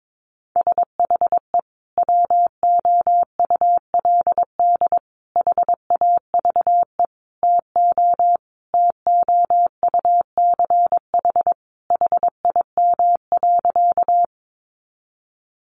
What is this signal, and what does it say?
Morse code, heard as a keyed tone, about 22 wpm, 700 Hz. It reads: SHE WOULD HAVE TO TOUCH HIM.
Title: Sentences from Top 1200 Words 22wpm